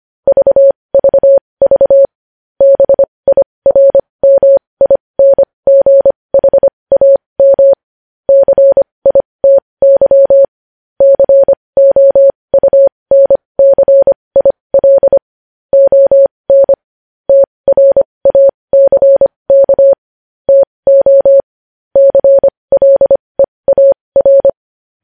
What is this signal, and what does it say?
Morse code, heard as a keyed tone, about 25 wpm, spaced as for 20 wpm, 550 Hz. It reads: VVV BIRMINGHAM CITY COUNCIL ON TRACK TO CLEAR
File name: podcast / News headlines-1x-25x20